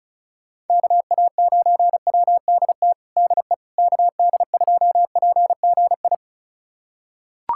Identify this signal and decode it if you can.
Morse code, heard as a keyed tone, about 36 wpm, 700 Hz. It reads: KA9WDT DE KD2PGI
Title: USA Callsign DE Callsign - 6 Characters - 35wpm